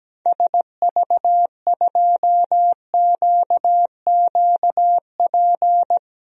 Morse code, heard as a keyed tone, about 17 wpm, 700 Hz. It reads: SV2QQP
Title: DXCC-100 Call Signs - 6 Characters - 17wpm